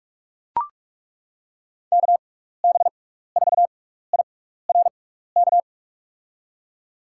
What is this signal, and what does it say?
Morse code, heard as a keyed tone, about 41 wpm, spaced as for 15 wpm, 700 Hz. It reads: KB4IRK
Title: USA Call Signs - 6 Characters - 6x Character Spacing - 45wpm